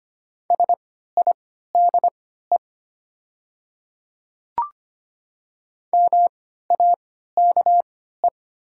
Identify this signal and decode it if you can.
Morse code, heard as a keyed tone, about 25 wpm, spaced as for 14 wpm, 700 Hz. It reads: SIDE MAKE
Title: 3x Character Spacing - Top 200 Words - 25wpm